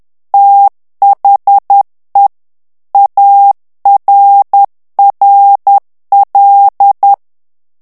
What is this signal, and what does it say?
Morse code, heard as a keyed tone, about 10 wpm, 800 Hz. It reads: THE ARRL
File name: ANGLAIS texte clair 3 vitesse 10 mots minute